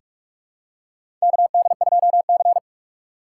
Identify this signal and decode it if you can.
Morse code, heard as a keyed tone, about 43 wpm, 700 Hz. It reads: KD2C
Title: USA Call Signs - 4 Characters - 6x Character Spacing - 45wpm